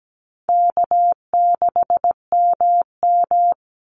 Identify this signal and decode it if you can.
Morse code, heard as a keyed tone, about 17 wpm, 700 Hz. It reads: K6MM
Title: Call Signs Repeated 2x - 17wpm